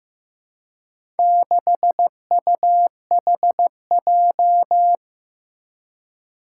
Morse code, heard as a keyed tone, about 15 wpm, 700 Hz. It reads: 6UHJ
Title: Four Challenging Letter-Number Combinations 15wpm